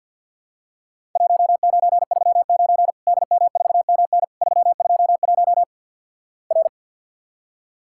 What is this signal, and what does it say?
Morse code, heard as a keyed tone, about 49 wpm, 700 Hz. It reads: 1939 DM4MN 321 R
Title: Worked All Europe DX Contest - QTC Exchange - 50wpm